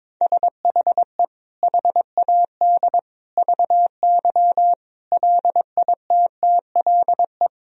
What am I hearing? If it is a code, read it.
Morse code, heard as a keyed tone, about 22 wpm, 700 Hz. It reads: SHE HAD VY LITTLE
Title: Sentences from Top 400 Words and Abbreviations 22wpm